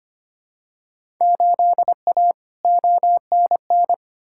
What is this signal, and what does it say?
Morse code, heard as a keyed tone, about 25 wpm, 700 Hz. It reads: 8A ONN
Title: Field Day Exchanges Encore - 25wpm